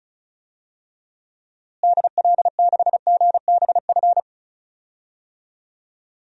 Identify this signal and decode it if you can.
Morse code, heard as a keyed tone, about 34 wpm, 700 Hz. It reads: DL6GBF
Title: DXCC-100 Call Signs - 6 Characters - Repeated 2x - 35wpm